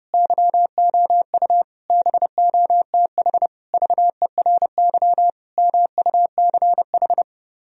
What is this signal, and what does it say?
Morse code, heard as a keyed tone, about 32 wpm, 700 Hz. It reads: YOU BOTH VERY MUCH